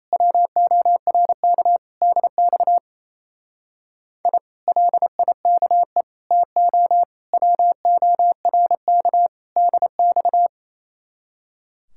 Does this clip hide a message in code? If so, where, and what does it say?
Yes: Morse code from 0.1 to 10.5 s, heard as a keyed tone, about 36 wpm, 700 Hz. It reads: WORK DX I LIKE TO WORK DX